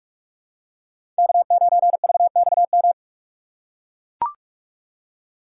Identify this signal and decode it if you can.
Morse code, heard as a keyed tone, about 41 wpm, 700 Hz. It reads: K9VXM
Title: USA Call Signs - 5 Characters - 4x Character Spacing - 45wpm